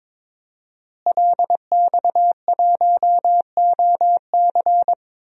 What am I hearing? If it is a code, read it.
Morse code, heard as a keyed tone, about 21 wpm, 700 Hz. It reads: LX1OC
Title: DXCC-100 Call Signs - 5 Characters - Repeated 2x - 22wpm